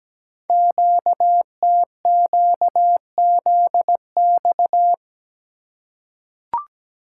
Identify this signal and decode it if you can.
Morse code, heard as a keyed tone, about 17 wpm, 700 Hz. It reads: QTQZX
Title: Code Groups - 5 Letters - 17wpm